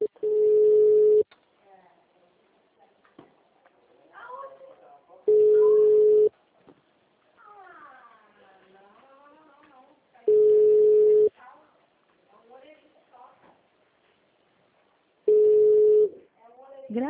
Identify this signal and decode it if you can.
Ringback tone (Europe, 425 Hz), with speech.